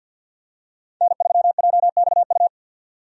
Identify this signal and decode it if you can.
Morse code, heard as a keyed tone, about 51 wpm, 700 Hz. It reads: N3JXU
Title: USA Call Signs - 5 Characters - 50wpm